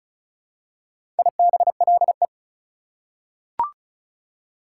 Morse code, heard as a keyed tone, about 36 wpm, 700 Hz. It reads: IBLE
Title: Prefixes and Suffixes 35wpm